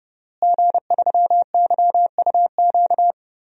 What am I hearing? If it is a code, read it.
Morse code, heard as a keyed tone, about 30 wpm, 700 Hz. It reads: G3YUQ
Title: DXCC-100 Call Signs - 5 Characters - Repeated 3x - 30wpm